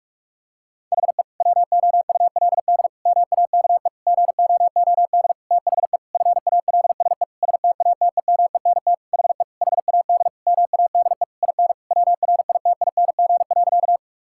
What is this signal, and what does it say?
Morse code, heard as a keyed tone, about 46 wpm, 700 Hz. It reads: HE WOULD MAKE GOOD THE FALSE STATEMENT HE HAD MADE IN WRITING.
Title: Sentences from Top 1600 Words 45wpm